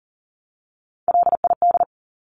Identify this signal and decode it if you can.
Morse code, heard as a keyed tone, about 40 wpm, 700 Hz. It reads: LID
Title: Abbreviations 40wpm